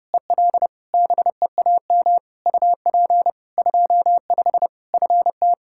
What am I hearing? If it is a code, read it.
Morse code, heard as a keyed tone, about 32 wpm, 700 Hz. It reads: EL BEAM UP 25 FT
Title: K1USN SST QSO Practice 30wpm